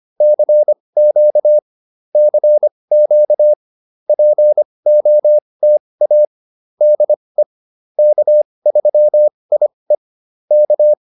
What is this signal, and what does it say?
Morse code, heard as a keyed tone, about 25 wpm, spaced as for 20 wpm, 590 Hz. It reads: CQ CQ POTA DE K3IE K